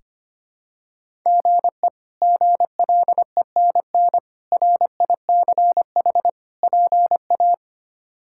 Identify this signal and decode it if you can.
Morse code, heard as a keyed tone, about 25 wpm, 700 Hz. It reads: GE GLENN RICH PA
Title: K1USN SST QSO Practice - 25wpm